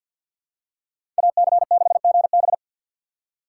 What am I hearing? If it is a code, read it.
Morse code, heard as a keyed tone, about 48 wpm, 700 Hz. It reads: AC6GB
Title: USA Call Signs - 2x Character Spacing - 50wpm